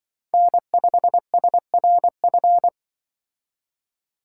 Morse code, heard as a keyed tone, about 23 wpm, 700 Hz. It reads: N5SRF
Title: Speed Racing - USA Call Signs 20wpm